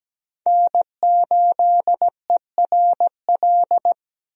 Morse code, heard as a keyed tone, about 17 wpm, 700 Hz. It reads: N8ERL
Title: USA Call Signs - 5 Characters - 17wpm